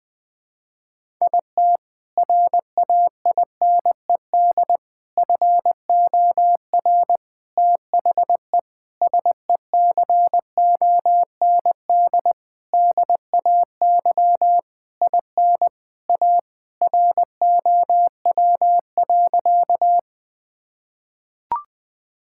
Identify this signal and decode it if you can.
Morse code, heard as a keyed tone, about 21 wpm, 700 Hz. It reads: IT RAINED FOR THE SECOND DAY IN A ROW.
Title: Sentences with 7-Word Phrases - 20wpm